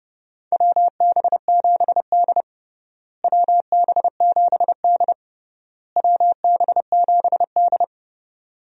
Morse code, heard as a keyed tone, about 30 wpm, 700 Hz. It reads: WB7D WB7D WB7D
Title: USA Call Signs - 4 Characters - Repeated 3x - 30wpm